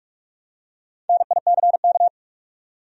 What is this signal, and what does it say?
Morse code, heard as a keyed tone, about 41 wpm, 700 Hz. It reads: NICK
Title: USA Common Names - 45wpm